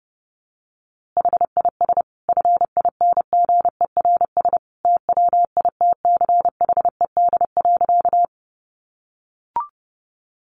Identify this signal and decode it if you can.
Morse code, heard as a keyed tone, about 30 wpm, 700 Hz. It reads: HIS FINGERS TWITCHED.
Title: A Sound Of Thunder 30wpm